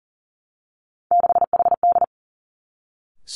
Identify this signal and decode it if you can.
Morse code, heard as a keyed tone, about 40 wpm, 700 Hz. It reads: 6HD
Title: Three Challenging Letter-Number Combinations 40wpm